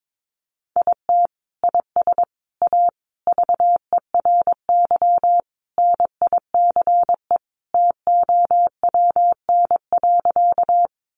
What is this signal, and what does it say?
Morse code, heard as a keyed tone, about 22 wpm, 700 Hz. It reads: IT IS A VERY NICE TOWN.
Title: Sentences from Top 1100 Words 22wpm